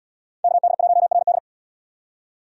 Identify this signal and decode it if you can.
Morse code, heard as a keyed tone, about 70 wpm, 700 Hz. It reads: KD1RD